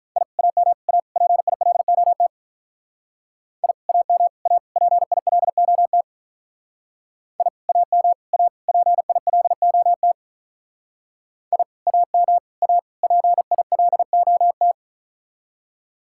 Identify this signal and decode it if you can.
Morse code, heard as a keyed tone, about 42 wpm, 700 Hz. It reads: I AM A PILOT I AM A PILOT I AM A PILOT I AM A PILOT
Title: Speed-Racing Common QSO Phrases - 35wpm